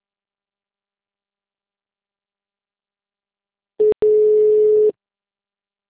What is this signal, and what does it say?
Ringback tone (Europe, 425 Hz).
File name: out- -2044